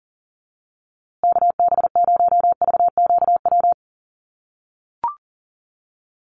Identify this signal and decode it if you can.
Morse code, heard as a keyed tone, about 40 wpm, 700 Hz. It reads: KB0VQW
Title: USA Call Signs Repeated 2x 40wpm